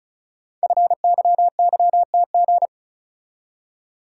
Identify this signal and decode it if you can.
Morse code, heard as a keyed tone, about 36 wpm, 690 Hz. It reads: FYYTG
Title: Code Groups - 5 Letters - 35wpm